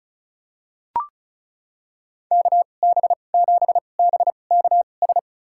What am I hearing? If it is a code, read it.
Morse code, heard as a keyed tone, about 36 wpm, spaced as for 25 wpm, 700 Hz. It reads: KB7BKS
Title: USA Call Signs - 6 Characters - 2x Character Spacing - 35wpm